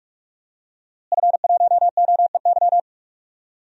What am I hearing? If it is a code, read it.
Morse code, heard as a keyed tone, about 45 wpm, 700 Hz. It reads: F1OEY
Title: DXCC-100 Call Signs - 5 Characters - Repeated 3x - 45wpm